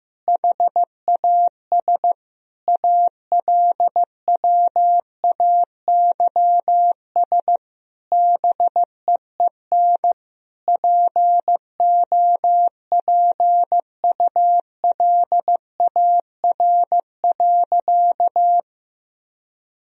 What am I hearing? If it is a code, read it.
Morse code, heard as a keyed tone, about 16 wpm, 700 Hz. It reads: HAS ALWAYS BEEN POPULAR.
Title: Sentences from Top 1000 Words - Review Part 3 - 15wpm